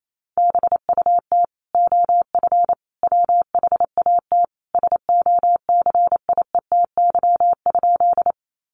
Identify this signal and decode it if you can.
Morse code, heard as a keyed tone, about 28 wpm, 700 Hz. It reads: BUT OF WHAT SOCIETY?
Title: Sentences from Top 700 Words 28wpm